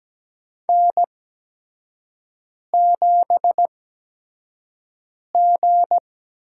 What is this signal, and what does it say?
Morse code, heard as a keyed tone, about 17 wpm, spaced as for 5 wpm, 700 Hz. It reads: N7G
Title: USA Call Signs - 5 Characters - 8x Character Spacing - 17wpm